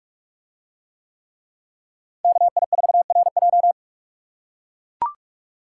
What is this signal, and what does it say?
Morse code, heard as a keyed tone, about 44 wpm, 700 Hz. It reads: KI4RJ